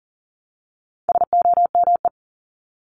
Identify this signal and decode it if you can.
Morse code, heard as a keyed tone, about 40 wpm, 700 Hz. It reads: SOME
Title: Top 500 Words 40wpm